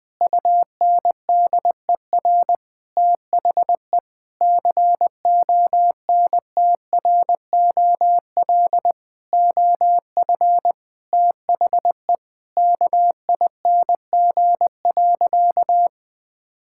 Morse code, heard as a keyed tone, about 20 wpm, 700 Hz. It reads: UNDER THE CONTROL OF THE KING.
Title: Sentences from Top 1000 Words - Review Part 5 - 20wpm